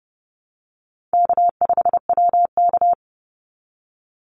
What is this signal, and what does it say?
Morse code, heard as a keyed tone, about 30 wpm, 700 Hz. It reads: K5WK
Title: Easy Call Signs Repeated 2x - 30wpm